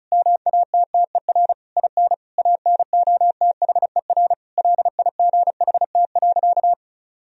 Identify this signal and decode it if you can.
Morse code, heard as a keyed tone, about 34 wpm, 700 Hz. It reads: MATTER IN ANOTHER LIGHT.
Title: Sentences from Top 1000 Words - Review Part 1 - 35wpm